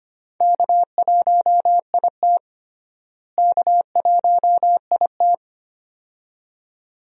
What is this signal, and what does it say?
Morse code, heard as a keyed tone, about 25 wpm, 700 Hz. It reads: K1IT K1IT
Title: USA Call Signs - 4 Characters - Repeated 2x - 25wpm